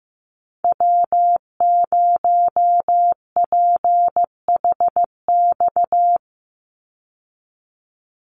Morse code, heard as a keyed tone, about 14 wpm, 700 Hz. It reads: W0PHX
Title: Call Signs Repeated 3x - 15wpm